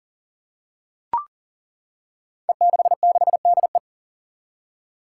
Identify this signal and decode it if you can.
Morse code, heard as a keyed tone, about 37 wpm, 700 Hz. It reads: E66DE
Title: Five Challenging Letter-Number Combinations 40wpm